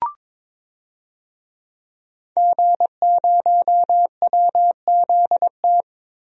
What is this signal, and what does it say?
Morse code, heard as a keyed tone, about 22 wpm, 700 Hz. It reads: G0WZT
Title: DXCC-100 Call Signs - 5 Characters - Repeated 2x - 22wpm